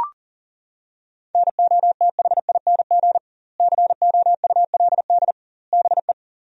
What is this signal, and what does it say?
Morse code, heard as a keyed tone, about 40 wpm, 700 Hz. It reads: NOTHING COULD BE
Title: 2x Word-Spacing - Sentences from Top 500 Words - 40wpm